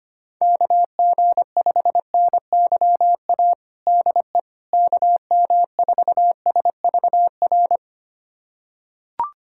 Morse code, heard as a keyed tone, about 25 wpm, 700 Hz. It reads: KG5NYA DE KM4SVR
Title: USA Callsign DE Callsign - 6 Characters - 25wpm